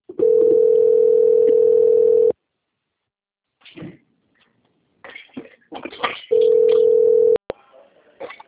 Ringback tone (North America), 440 + 480 Hz.
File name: out- - - - - .6464